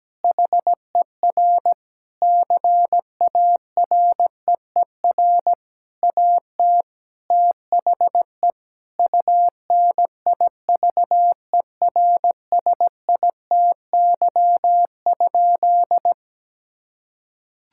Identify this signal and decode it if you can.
Morse code, heard as a keyed tone, about 17 wpm, 700 Hz. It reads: HER CAREER AT THE UNIVERSITY?
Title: Sentences from Top 1000 Words - Review Part 6 - 17wpm